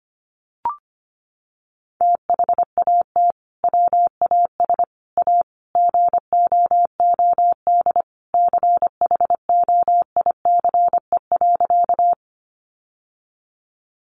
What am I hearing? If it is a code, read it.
Morse code, heard as a keyed tone, about 25 wpm, 700 Hz. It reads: THAT WAS A GOOD CHOICE.
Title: Sentences from Top 900 Words 25wpm